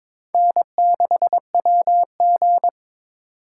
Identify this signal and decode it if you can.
Morse code, heard as a keyed tone, about 22 wpm, 700 Hz. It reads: N6WG
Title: USA Call Signs - 8x Character Spacing - 22wpm